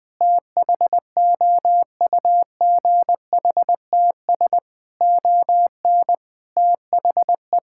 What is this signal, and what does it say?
Morse code, heard as a keyed tone, about 20 wpm, 700 Hz. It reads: THOUGHTS ON THE
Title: Sets of 6 Words - 20wpm